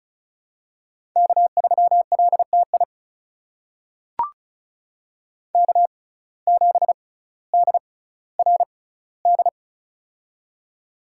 Morse code, heard as a keyed tone, about 36 wpm, spaced as for 18 wpm, 700 Hz. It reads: K3LTI K7DRD
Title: USA Call Signs - 5 Characters - 6x Character Spacing - 35wpm